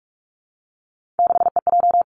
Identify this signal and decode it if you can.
Morse code, heard as a keyed tone, about 46 wpm, 700 Hz. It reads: 6EJ
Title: Three Challenging Letter-Number Combinations 45wpm